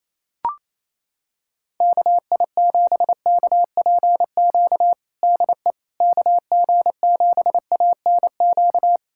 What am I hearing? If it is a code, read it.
Morse code, heard as a keyed tone, about 28 wpm, 700 Hz. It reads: KI7KPQ DE KG7ANQ
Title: USA Callsign DE Callsign - 6 Characters - 28wpm